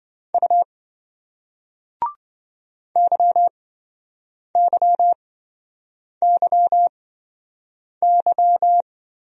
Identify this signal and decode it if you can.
Morse code, heard as a keyed tone, about 27 wpm, spaced as for 7 wpm, 700 Hz. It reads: U YYYY